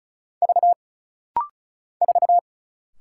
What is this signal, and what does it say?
Morse code, heard as a keyed tone, about 36 wpm, 700 Hz. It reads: V 4
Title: Single Letter-Number - Rapid-Fire 35wpm